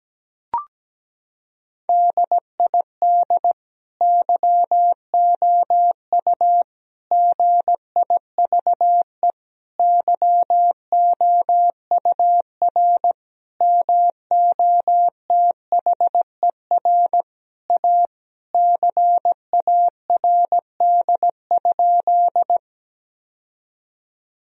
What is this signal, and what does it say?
Morse code, heard as a keyed tone, about 17 wpm, 700 Hz. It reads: DID YOU GIVE YOUR MOTHER A CARD?
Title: Sentences from Top 1000 Words - Review Part 4 - 17wpm